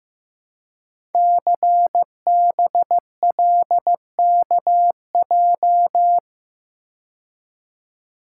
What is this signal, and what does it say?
Morse code, heard as a keyed tone, about 15 wpm, 700 Hz. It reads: CBLKJ
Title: Code Groups - 5 Letters - 15wpm